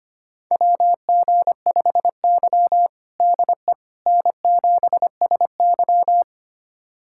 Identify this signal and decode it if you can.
Morse code, heard as a keyed tone, about 25 wpm, 700 Hz. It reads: WG5Y DE N7SY
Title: USA Callsign DE Callsign - 4 Characters - 25wpm